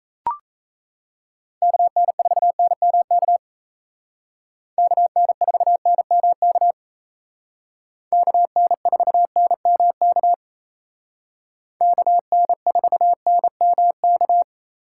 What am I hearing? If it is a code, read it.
Morse code, heard as a keyed tone, about 35 wpm, 700 Hz. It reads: KN4NMK KN4NMK KN4NMK KN4NMK
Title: Speed Racing - USA 2x3 Call Signs 28wpm